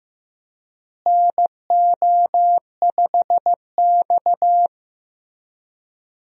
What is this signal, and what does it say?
Morse code, heard as a keyed tone, about 15 wpm, 700 Hz. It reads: NO5X
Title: USA Call Signs - 4 Characters - Repeated 2x - 15wpm